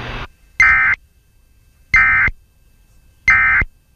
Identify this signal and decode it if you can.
Emergency Alert System (SAME) tones, heard as harsh buzzing bursts. It is heard from 0.6 to 3.6 s.
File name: KHFI-EAS